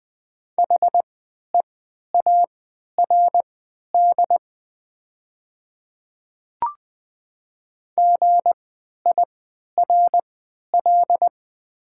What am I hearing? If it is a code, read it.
Morse code, heard as a keyed tone, about 20 wpm, spaced as for 11 wpm, 700 Hz. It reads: HEARD GIRL
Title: 3x Character Spacing - Top 500 Words - 20wpm